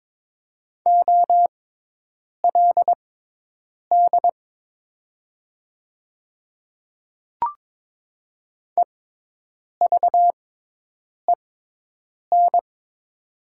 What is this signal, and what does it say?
Morse code, heard as a keyed tone, about 22 wpm, spaced as for 8 wpm, 700 Hz. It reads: OLD EVEN
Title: 6x Character Spacing - Top 200 Words - 22wpm